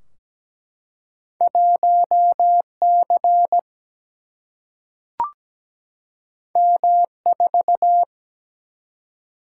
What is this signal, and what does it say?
Morse code, heard as a keyed tone, about 17 wpm, 700 Hz. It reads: 1C M4
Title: Single-Letter Single-Number 17wpm